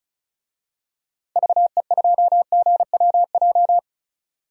Morse code, heard as a keyed tone, about 34 wpm, 700 Hz. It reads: VE2GWJ
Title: DXCC-100 Call Signs - 6 Characters - 35wpm